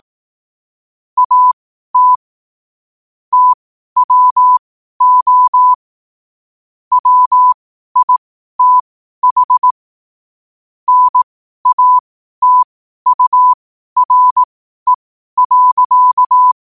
Morse code, heard as a keyed tone, about 18 wpm, spaced as for 12 wpm, 1000 Hz. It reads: AT TWO WITH NATURE.
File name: qotdfiles / 12wpm